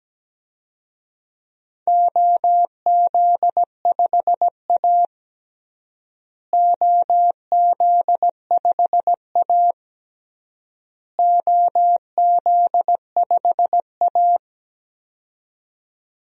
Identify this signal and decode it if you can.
Morse code, heard as a keyed tone, about 17 wpm, 700 Hz. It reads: OZ5A OZ5A OZ5A
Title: DXCC-100 Call Signs - 4 Characters - Repeated 3x - 17wpm